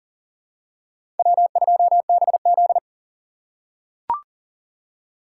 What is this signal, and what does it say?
Morse code, heard as a keyed tone, about 40 wpm, 700 Hz. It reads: W2BZ
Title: USA Call Signs - 4 Characters - 4x Character Spacing - 40wpm